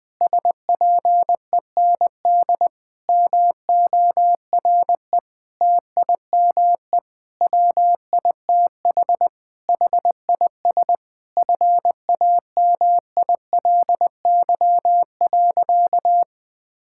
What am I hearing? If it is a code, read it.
Morse code, heard as a keyed tone, about 21 wpm, 700 Hz. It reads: SPEND MORE TIME WITH HIS FAMILY.
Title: Sentences with 7-Word Phrases - 20wpm